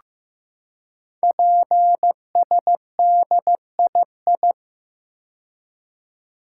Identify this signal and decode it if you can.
Morse code, heard as a keyed tone, about 15 wpm, 700 Hz. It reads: PSDII
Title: Code Groups - 5 Letter:Numbers - 15wpm